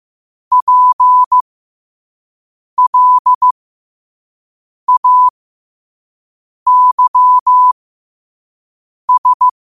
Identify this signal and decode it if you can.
Morse code, heard as a keyed tone, about 15 wpm, spaced as for 5 wpm, 1000 Hz. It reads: PLAYS
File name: 2024-10-07-5wpm